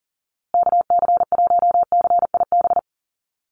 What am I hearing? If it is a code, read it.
Morse code, heard as a keyed tone, about 40 wpm, 700 Hz. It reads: KC1CIB
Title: USA Call Signs Repeated 3x 40wpm